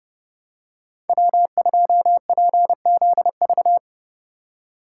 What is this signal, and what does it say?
Morse code, heard as a keyed tone, about 30 wpm, 700 Hz. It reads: W2PZV
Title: USA Call Signs - 5 Characters - 4x Character Spacing - 30wpm